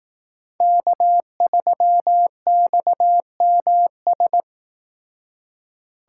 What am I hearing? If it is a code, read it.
Morse code, heard as a keyed tone, about 18 wpm, 700 Hz. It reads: K3XMS
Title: Speed Racing - USA 1x3 Call Signs 15wpm